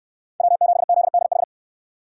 Morse code, heard as a keyed tone, about 68 wpm, 700 Hz. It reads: W7ZDH